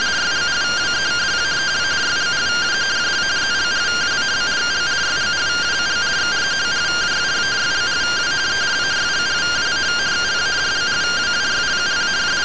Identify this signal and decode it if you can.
Two-tone teleprinter signal, about 170 Hz shift.